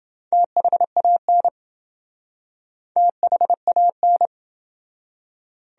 Morse code, heard as a keyed tone, about 28 wpm, 700 Hz. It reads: THAN THAN